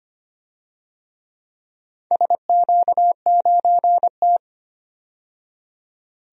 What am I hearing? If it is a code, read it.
Morse code, heard as a keyed tone, about 25 wpm, 700 Hz. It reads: SQ9T
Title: DXCC-100 Call Signs - 4 Characters - 25wpm